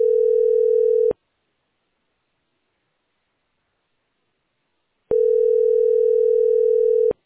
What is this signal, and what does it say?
Ringback tone (North America), 440 + 480 Hz.